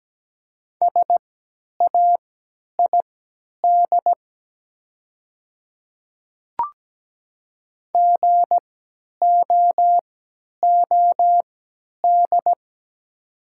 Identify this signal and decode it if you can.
Morse code, heard as a keyed tone, about 17 wpm, spaced as for 10 wpm, 700 Hz. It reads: SAID GOOD
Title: 3x Character Spacing - Top 200 Words - 17wpm